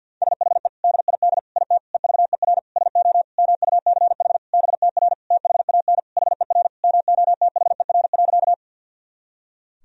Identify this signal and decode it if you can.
Morse code, heard as a keyed tone, about 49 wpm, 700 Hz. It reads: SHE DID IT EVER SO MUCH BTR THAN HER MOTHER.
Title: Sentences from Top 400 Words and Abbreviations 50wpm